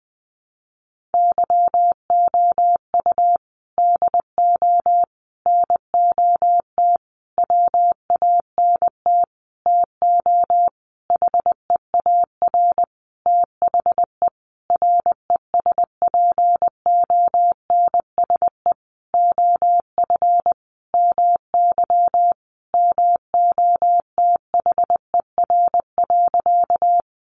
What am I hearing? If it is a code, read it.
Morse code, heard as a keyed tone, about 19 wpm, 700 Hz. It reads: YOU DO NOT WANT TO HEAR THE RESPONSE OF MY MOTHER.
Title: Sentences from Top 1400 Words 20wpm